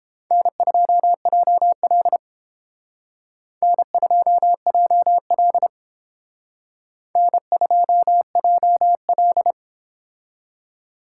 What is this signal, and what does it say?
Morse code, heard as a keyed tone, about 31 wpm, 700 Hz. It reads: N2JL N2JL N2JL